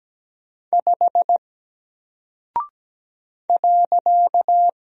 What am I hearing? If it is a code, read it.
Morse code, heard as a keyed tone, about 17 wpm, 700 Hz. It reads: 5 .
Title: Single Character (Letter:Number:Punctuation:Prosign) 17wpm